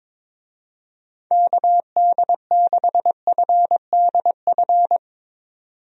Morse code, heard as a keyed tone, about 22 wpm, 700 Hz. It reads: KD6FDF
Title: USA Call Signs - 6 Characters - 22wpm